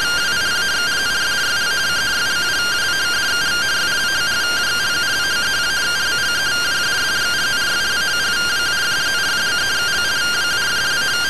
Two-tone teleprinter signal, about 170 Hz shift.
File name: findings / CT AXIAL -377.gen-18 rtty rx